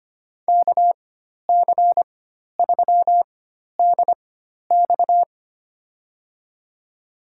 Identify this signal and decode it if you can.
Morse code, heard as a keyed tone, about 25 wpm, spaced as for 12 wpm, 700 Hz. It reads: KC3DX
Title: USA Call Signs - 4x Character Spacing - 25wpm